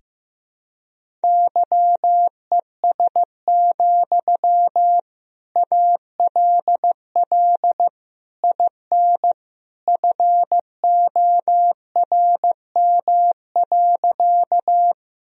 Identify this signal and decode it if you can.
Morse code, heard as a keyed tone, about 15 wpm, 700 Hz. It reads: YES, ALL IN FORM.
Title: Sentences from Top 1000 Words - Review Part 1 - 15wpm